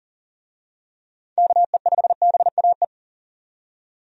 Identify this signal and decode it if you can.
Morse code, heard as a keyed tone, about 40 wpm, 700 Hz. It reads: KE5BAE
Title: USA Call Signs - 6 Characters - 40wpm